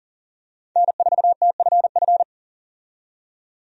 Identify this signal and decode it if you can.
Morse code, heard as a keyed tone, about 40 wpm, 700 Hz. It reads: N4TFF
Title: USA Call Signs - 5 Characters - 40wpm